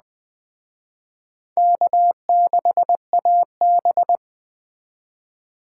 Morse code, heard as a keyed tone, about 21 wpm, 700 Hz. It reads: K6AB
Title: Speed Racing - USA Call Signs 17wpm